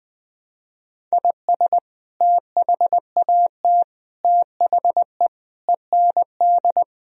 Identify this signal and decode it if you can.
Morse code, heard as a keyed tone, about 20 wpm, 700 Hz. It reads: IS THAT THE END
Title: Sentences from Top 1000 Words - Review Part 1 - 20wpm